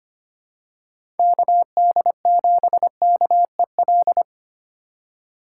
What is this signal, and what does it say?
Morse code, heard as a keyed tone, about 25 wpm, 700 Hz. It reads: KD7KEL
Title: USA Call Signs - 4x Character Spacing - 25wpm